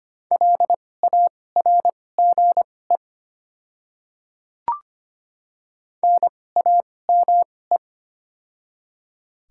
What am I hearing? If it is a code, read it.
Morse code, heard as a keyed tone, about 25 wpm, spaced as for 18 wpm, 690 Hz. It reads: LARGE NAME
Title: 2x Character Spacing - Top 200 Words - 25wpm